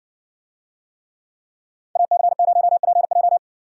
Morse code, heard as a keyed tone, about 60 wpm, 700 Hz. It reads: AC0PJ